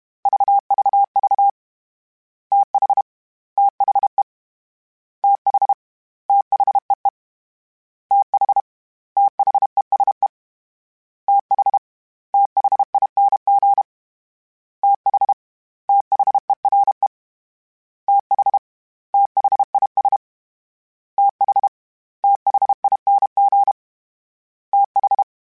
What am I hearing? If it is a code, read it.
Morse code, heard as a keyed tone, about 32 wpm, 800 Hz. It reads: VVV TH THE TH THEE TH THESE TH THING TH THERE TH THIS TH THING TH
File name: THs / TH / TH Prefixes 1 @30